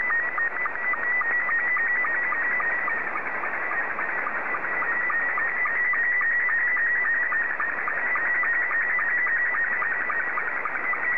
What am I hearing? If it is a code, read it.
SSTV (slow-scan television), Scottie 2.
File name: sstv / S2-JA3QJQ